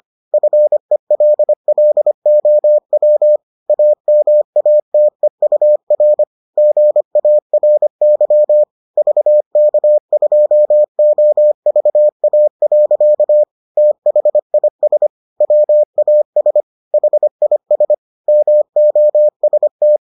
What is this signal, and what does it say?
Morse code, heard as a keyed tone, about 25 wpm, 600 Hz. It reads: FELLOW AMATEUR GARY VK2OVA. THIS WAS HIS MOST